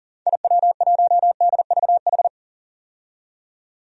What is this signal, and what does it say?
Morse code, heard as a keyed tone, about 37 wpm, 700 Hz. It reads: IW1DVH
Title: DXCC-100 Call Signs - 6 Characters - 40wpm